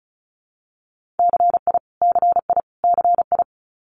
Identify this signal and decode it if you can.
Morse code, heard as a keyed tone, about 36 wpm, 700 Hz. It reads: CI CI CI
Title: Multi-letter Phonemes Repeated 3x 35wpm